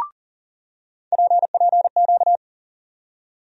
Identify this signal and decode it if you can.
Morse code, heard as a keyed tone, about 40 wpm, 700 Hz. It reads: PPQ
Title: USA Call Sign Postfixes - 3 Letters 40wpm